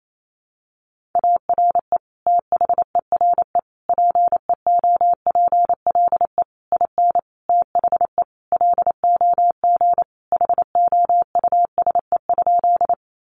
Morse code, heard as a keyed tone, about 28 wpm, 700 Hz. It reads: ARE THERE PEOPLE IN THE LOG HOUSE?
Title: Sentences from Top 1400 Words 28wpm